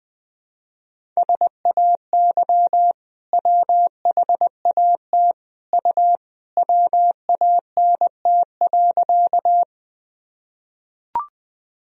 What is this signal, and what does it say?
Morse code, heard as a keyed tone, about 20 wpm, 700 Hz. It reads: SAY WHAT U WANT.
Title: Sentences from Top 200 Words and Abbreviations 20wpm